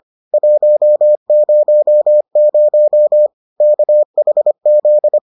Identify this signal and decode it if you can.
Morse code, heard as a keyed tone, about 25 wpm, 600 Hz. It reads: 100 KHZ